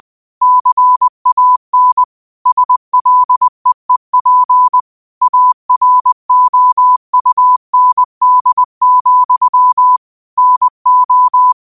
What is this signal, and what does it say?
Morse code, heard as a keyed tone, about 20 wpm, 1000 Hz. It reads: CAN SLEEP AROUND, NO